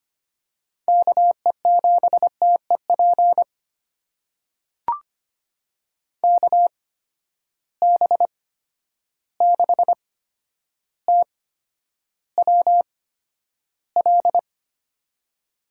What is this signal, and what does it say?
Morse code, heard as a keyed tone, about 25 wpm, spaced as for 11 wpm, 700 Hz. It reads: KE7TEP KB6TWL
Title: USA Call Signs - 6 Characters - 8x Character Spacing - 25wpm